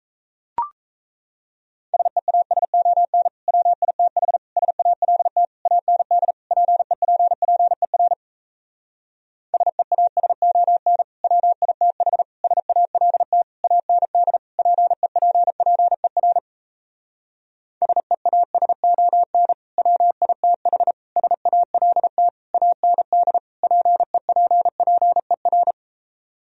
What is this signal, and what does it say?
Morse code, heard as a keyed tone, about 38 wpm, 700 Hz. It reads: SEASON WITH SALT AND PEPPER SEASON WITH SALT AND PEPPER SEASON WITH SALT AND PEPPER